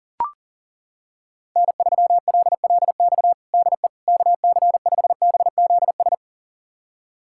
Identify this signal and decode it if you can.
Morse code, heard as a keyed tone, about 37 wpm, 700 Hz. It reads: N3LLX DE KC5BZS
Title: USA Callsign DE Callsign 40wpm